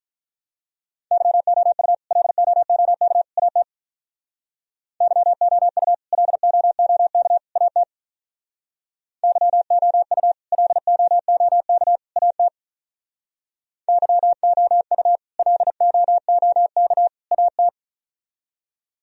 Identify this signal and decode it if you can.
Morse code, heard as a keyed tone, about 45 wpm, 700 Hz. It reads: YOU LOOK AT YOU LOOK AT YOU LOOK AT YOU LOOK AT